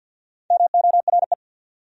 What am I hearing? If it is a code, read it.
Morse code, heard as a keyed tone, about 66 wpm, 700 Hz. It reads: MORE